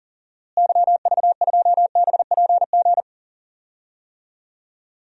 Morse code, heard as a keyed tone, about 40 wpm, 700 Hz. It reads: YV2BPG
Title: DXCC-100 Call Signs - 6 Characters - Repeated 2x - 40wpm